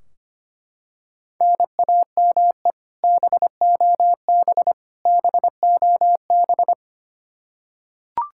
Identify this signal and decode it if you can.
Morse code, heard as a keyed tone, about 25 wpm, 700 Hz. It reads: NAME BOB BOB
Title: Practice QSO - Exchange 1 - Beginning - 25wpm